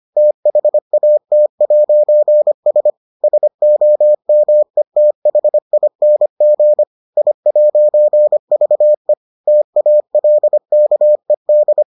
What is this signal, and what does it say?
Morse code, heard as a keyed tone, about 25 wpm, 600 Hz. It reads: THAT'S SOMETHING I'VE TALKED